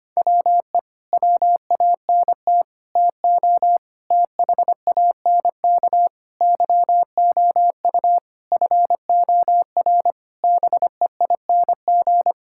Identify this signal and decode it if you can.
Morse code, heard as a keyed tone, about 25 wpm, 700 Hz. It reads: WE WANT TO THANK YOU FOR BEING
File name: Sets of 9 Words - 25wpm